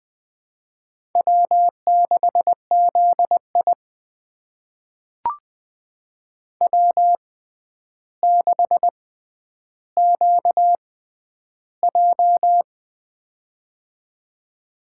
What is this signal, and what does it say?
Morse code, heard as a keyed tone, about 20 wpm, spaced as for 10 wpm, 700 Hz. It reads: W6ZI W6QJ
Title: USA Call Signs - 4 Characters - 6x Character Spacing - 20wpm